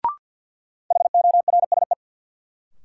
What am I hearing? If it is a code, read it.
Morse code, heard as a keyed tone, about 47 wpm, 700 Hz. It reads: HORSE
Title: Sentences from Top 300 Words - New Word Drill - 50wpm